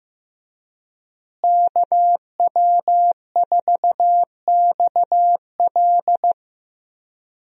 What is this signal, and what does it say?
Morse code, heard as a keyed tone, about 15 wpm, 700 Hz. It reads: KW4XL
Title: USA Call Signs - 5 Characters - Repeated 3x - 15wpm